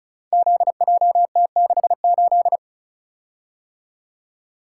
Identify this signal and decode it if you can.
Morse code, heard as a keyed tone, about 35 wpm, 700 Hz. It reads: ZJT68
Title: Code Groups - 5 Letter:Numbers - 35wpm